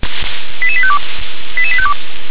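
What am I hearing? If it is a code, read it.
Five-tone selective call (ZVEI), heard as quick runs of five tones, tones of about 70 ms. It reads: 88852 88852
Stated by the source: ZVEI five-tone selective call, address 88126, received by SDR.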